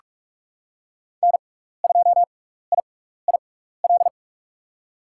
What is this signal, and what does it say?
Morse code, heard as a keyed tone, about 47 wpm, spaced as for 16 wpm, 700 Hz. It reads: N2IIL
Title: USA Call Signs - 5 Characters - 6x Character Spacing - 45wpm